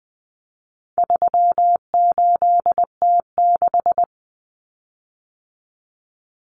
Morse code, heard as a keyed tone, about 20 wpm, 700 Hz. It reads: 38T6
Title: Common Cut Numbers - 4-Digit 20wpm